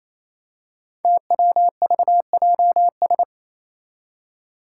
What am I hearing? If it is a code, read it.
Morse code, heard as a keyed tone, about 27 wpm, 700 Hz. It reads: TWVJS
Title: Five Challenging Letter-Number Combinations 28wpm